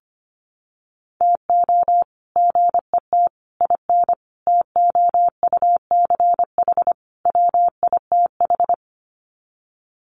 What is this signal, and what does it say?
Morse code, heard as a keyed tone, about 25 wpm, 700 Hz. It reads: TO GET IN TOUCH WITH
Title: Sets of 5 Words Encore2 25wpm